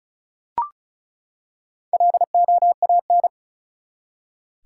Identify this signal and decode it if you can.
Morse code, heard as a keyed tone, about 36 wpm, 700 Hz. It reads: LOAN
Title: Four Letter Words - 35wpm